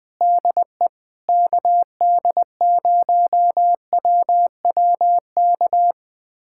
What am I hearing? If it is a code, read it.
Morse code, heard as a keyed tone, about 20 wpm, 700 Hz. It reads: DE KD0WWK
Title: USA Callsign DE Callsign 20wpm